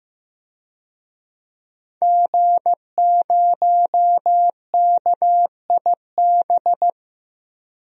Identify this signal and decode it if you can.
Morse code, heard as a keyed tone, about 15 wpm, 700 Hz. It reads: G0KIB
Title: DXCC-100 Call Signs - 5 Characters - 15wpm